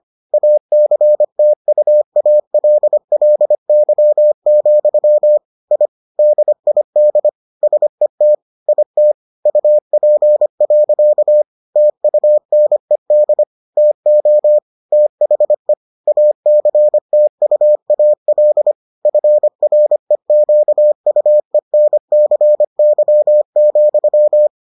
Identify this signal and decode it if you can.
Morse code, heard as a keyed tone, about 25 wpm, 600 Hz. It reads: ACTUALLY, I DID SET IT UP. TUNED TO THE ACTUAL FREQUENCY,